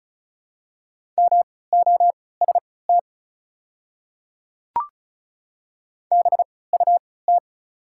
Morse code, heard as a keyed tone, about 36 wpm, spaced as for 20 wpm, 700 Hz. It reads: MOST BUT
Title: 3x Character Spacing - Top 100 Words - 35wpm